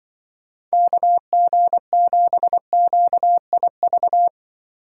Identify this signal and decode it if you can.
Morse code, heard as a keyed tone, about 25 wpm, 700 Hz. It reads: KG7QIV
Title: Speed Racing - USA 2x3 Call Signs 20wpm